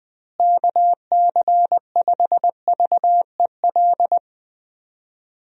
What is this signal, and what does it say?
Morse code, heard as a keyed tone, about 20 wpm, 700 Hz. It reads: KC5VEL
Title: USA Call Signs - 6 Characters - Repeated 3x - 20wpm